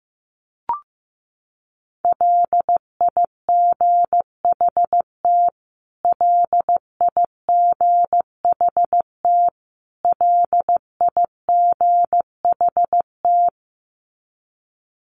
Morse code, heard as a keyed tone, about 15 wpm, 700 Hz. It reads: LIGHT LIGHT LIGHT
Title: Top 500 Words Repeated 15wpm